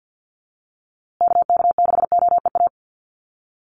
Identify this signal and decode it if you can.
Morse code, heard as a keyed tone, about 50 wpm, 700 Hz. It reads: KK6OEA
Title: USA Call Signs Repeated 2x 50wpm